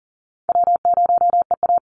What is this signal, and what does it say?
Morse code, heard as a keyed tone, about 40 wpm, 700 Hz. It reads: W0EA
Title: Easy Call Signs - 40wpm